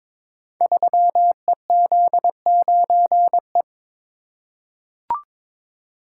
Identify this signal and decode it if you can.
Morse code, heard as a keyed tone, about 22 wpm, 700 Hz. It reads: 3EZ9E
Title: Code Groups - 5 Letter:Numbers - 22wpm